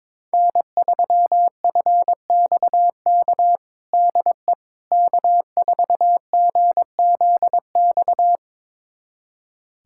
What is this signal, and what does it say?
Morse code, heard as a keyed tone, about 22 wpm, 700 Hz. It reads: N3FXK DE K4GZX
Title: USA Callsign DE Callsign - 5 Characters - 22wpm